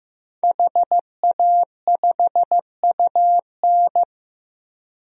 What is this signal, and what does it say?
Morse code, heard as a keyed tone, about 15 wpm, 700 Hz. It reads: HA5UN
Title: DXCC-100 Call Signs - 5 Characters - 15wpm